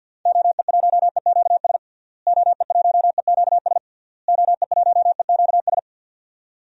Morse code, heard as a keyed tone, about 47 wpm, 700 Hz. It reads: OE1EQS OE1EQS OE1EQS
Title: DXCC-100 Call Signs - 6 Characters - Repeated 3x - 50wpm